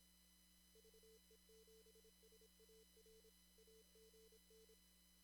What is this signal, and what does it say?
Morse code, heard as a keyed tone, about 25 wpm, 440 Hz. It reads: VE7SAR AGN
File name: CW / kt5j wpx cw-op-k5pi